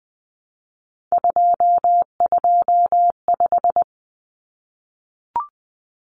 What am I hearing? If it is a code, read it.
Morse code, heard as a keyed tone, about 20 wpm, 700 Hz. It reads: 225
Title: Common Cut Numbers - 3-Digit 20wpm